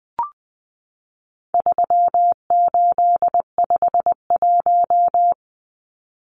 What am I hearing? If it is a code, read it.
Morse code, heard as a keyed tone, about 20 wpm, 700 Hz. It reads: 3851
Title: Numbers - 4-Digit 20wpm